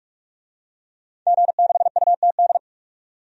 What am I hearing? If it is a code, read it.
Morse code, heard as a keyed tone, about 46 wpm, 700 Hz. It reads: G6UTD
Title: DXCC-100 Call Signs - 5 Characters - 45wpm